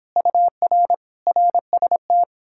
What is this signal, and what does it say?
Morse code, heard as a keyed tone, about 29 wpm, 700 Hz. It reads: UR RST